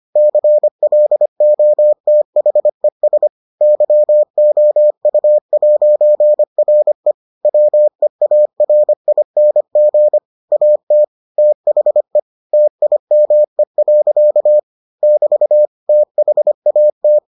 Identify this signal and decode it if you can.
Morse code, heard as a keyed tone, about 25 wpm, 600 Hz. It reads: CLOTHES YOU'RE WEARING AT THE TIME. <BT> THAT